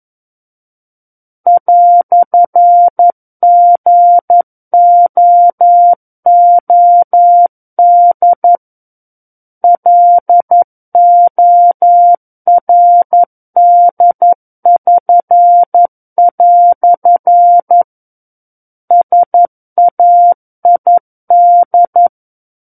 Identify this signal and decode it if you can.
Morse code, heard as a keyed tone, about 11 wpm, 700 Hz. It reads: "GOOD LORD<SN>" SAID
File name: 11 WPM